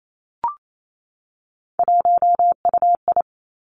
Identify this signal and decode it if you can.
Morse code, heard as a keyed tone, about 28 wpm, 700 Hz. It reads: 1UI
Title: Three Challenging Letter-Number Combinations 28wpm